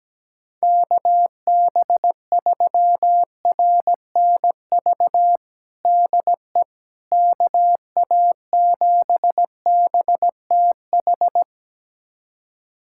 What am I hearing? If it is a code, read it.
Morse code, heard as a keyed tone, about 17 wpm, 700 Hz. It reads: KB3RNV DE KA7BTH
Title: USA Callsign DE Callsign 17wpm